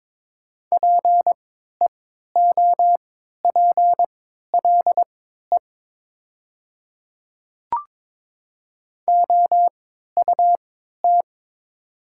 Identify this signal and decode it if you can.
Morse code, heard as a keyed tone, about 22 wpm, spaced as for 12 wpm, 700 Hz. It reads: PEOPLE OUT
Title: 3x Character Spacing - Top 200 Words - 22wpm